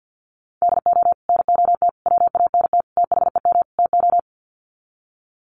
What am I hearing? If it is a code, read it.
Morse code, heard as a keyed tone, about 50 wpm, 700 Hz. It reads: DO NOT WANT THEM TO
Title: Sets of 5 Words Encore 50wpm